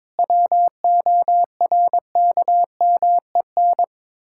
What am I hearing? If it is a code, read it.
Morse code, heard as a keyed tone, about 22 wpm, 700 Hz. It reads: WORKMEN
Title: Pangram Challenge v2 - 22wpm